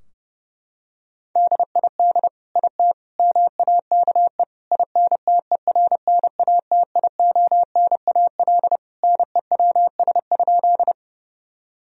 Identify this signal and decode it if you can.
Morse code, heard as a keyed tone, about 30 wpm, 700 Hz. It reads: DID IT MAKE INTERNATIONAL NEWS?
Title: Sentences from Top 1000 Words - Review Part 4 - 30wpm